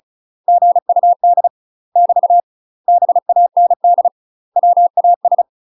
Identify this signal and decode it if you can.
Morse code, heard as a keyed tone, about 34 wpm, 700 Hz. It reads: GUD <BT> BAND WAS
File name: UK MASTER QSOs TRAINING / 35WPM / UK MASTER QSOs 0001